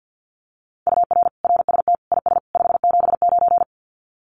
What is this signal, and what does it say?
Morse code, heard as a keyed tone, about 50 wpm, 700 Hz. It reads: UR RST IS 579
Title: Common QSO Phrases 50wpm